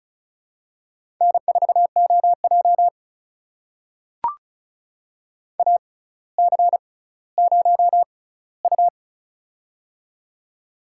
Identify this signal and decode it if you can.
Morse code, heard as a keyed tone, about 36 wpm, spaced as for 18 wpm, 700 Hz. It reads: N4OJ AC0U
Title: USA Call Signs - 4 Characters - 6x Character Spacing - 35wpm